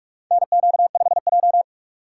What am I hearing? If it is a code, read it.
Morse code, heard as a keyed tone, about 47 wpm, 700 Hz. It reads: NQ5J
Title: USA Call Signs - 4 Characters - 2x Character Spacing - 45wpm